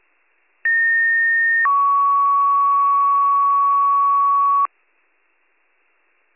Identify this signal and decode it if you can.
Two-tone page, heard as a one-second tone, then a three-second tone, about 1850 then 1120 Hz. It is heard from 0.6 to 4.7 s.